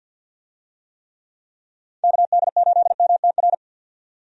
Morse code, heard as a keyed tone, about 48 wpm, 700 Hz. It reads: KD8MTR
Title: USA Call Signs - 6 Characters - 50wpm